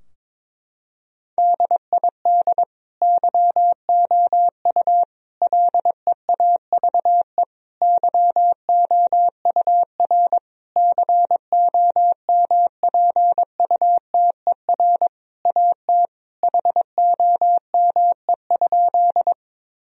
Morse code, heard as a keyed tone, about 21 wpm, 700 Hz. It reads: DID YOU LEAVE YOUR COMPUTER AT HOME?
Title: Sentences from Top 1000 Words - Review Part 3 - 22wpm